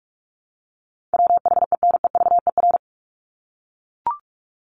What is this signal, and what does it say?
Morse code, heard as a keyed tone, about 44 wpm, 700 Hz. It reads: WHENEVER
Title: Top 1001-1250 Words 45wpm